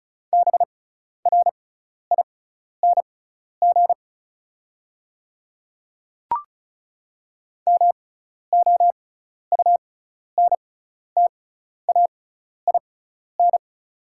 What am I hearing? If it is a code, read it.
Morse code, heard as a keyed tone, about 34 wpm, spaced as for 12 wpm, 700 Hz. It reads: BRING MOUNTAIN
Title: 6x Character Spacing - Top 500 Words - 35wpm